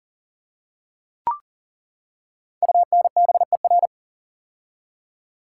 Morse code, heard as a keyed tone, about 39 wpm, 700 Hz. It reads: UNBER
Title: Code Groups - 5 Letters - 40wpm